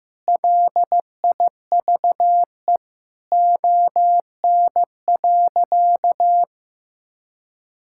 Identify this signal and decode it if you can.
Morse code, heard as a keyed tone, about 15 wpm, 700 Hz. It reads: LIVE ON.
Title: Sentences from Top 1000 Words - Review Part 6 - 15wpm